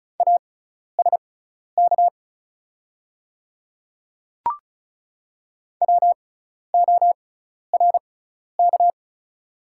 Morse code, heard as a keyed tone, about 36 wpm, spaced as for 12 wpm, 700 Hz. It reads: ASK WORK
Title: 6x Character Spacing - Top 500 Words - 35wpm